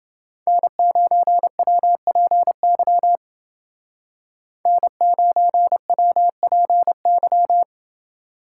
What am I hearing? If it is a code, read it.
Morse code, heard as a keyed tone, about 28 wpm, 700 Hz. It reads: N9WPY N9WPY